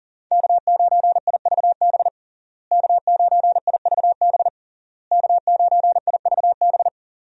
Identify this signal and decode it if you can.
Morse code, heard as a keyed tone, about 40 wpm, 700 Hz. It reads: K9IVB K9IVB K9IVB
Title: USA Call Signs - 5 Characters - Repeated 3x - 40wpm